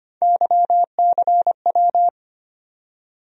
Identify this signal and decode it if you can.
Morse code, heard as a keyed tone, about 25 wpm, 700 Hz. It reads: YCW
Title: USA Call Sign Postfixes - 3 Letters 25wpm